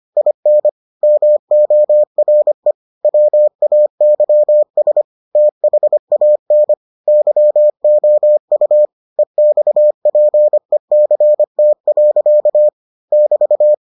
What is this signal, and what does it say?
Morse code, heard as a keyed tone, about 25 wpm, 600 Hz. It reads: IN MORE WAYS THAN YOU EXPECT. <BT>